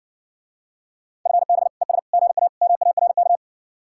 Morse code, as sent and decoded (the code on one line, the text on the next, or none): ..-. -... / . ... / --. .- / -- .- .-. -.-
FB ES GA MARK